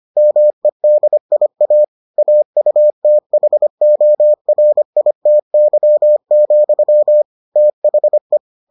Morse code, heard as a keyed tone, about 25 wpm, 600 Hz. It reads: MEDIA AUTHORITY, THE